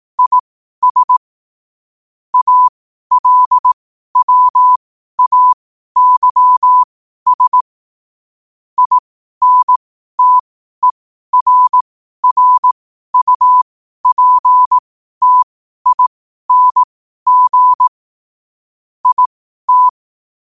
Morse code, heard as a keyed tone, about 18 wpm, spaced as for 12 wpm, 1000 Hz. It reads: IS ALWAYS INTERRUPTING IT
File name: qotdfiles / 12wpm